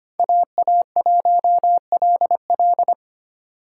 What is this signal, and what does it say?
Morse code, heard as a keyed tone, about 25 wpm, 700 Hz. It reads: AA1LL
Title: USA Call Signs - 5 Characters - 6x Character Spacing - 25wpm